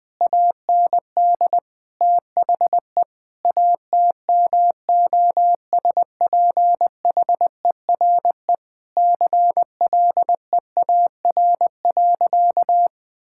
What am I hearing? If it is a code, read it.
Morse code, heard as a keyed tone, about 20 wpm, 700 Hz. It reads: AND THE ATMOSPHERE CLEAR.
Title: Sentences from Top 1600 Words 20wpm